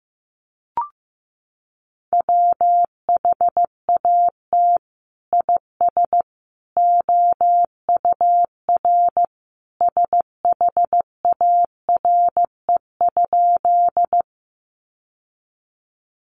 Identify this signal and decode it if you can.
Morse code, heard as a keyed tone, about 15 wpm, 700 Hz. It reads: WHAT IS OUR SHARE?
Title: Sentences from Top 900 Words 15wpm